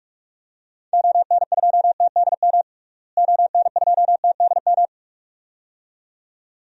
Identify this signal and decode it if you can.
Morse code, heard as a keyed tone, about 42 wpm, 700 Hz. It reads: ON2TDM ON2TDM
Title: DXCC-100 Call Signs - 6 Characters - Repeated 2x - 45wpm